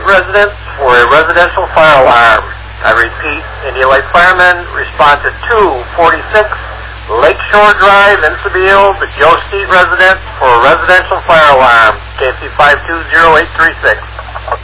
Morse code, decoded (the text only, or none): none